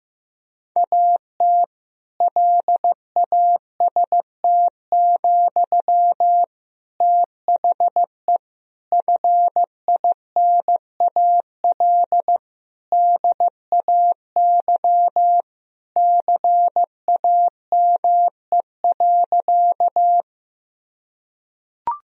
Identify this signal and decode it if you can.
Morse code, heard as a keyed tone, about 15 wpm, 700 Hz. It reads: AT LAST, THE FINAL DAY CAME.
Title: Sentences from Top 1000 Words - Review Part 2 - 15wpm